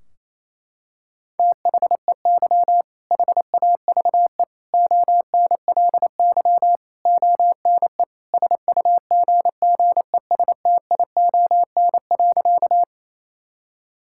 Morse code, heard as a keyed tone, about 27 wpm, 700 Hz. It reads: THEY HAVE ONLY ONE SUGGESTION.
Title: Sentences from Top 1600 Words 28wpm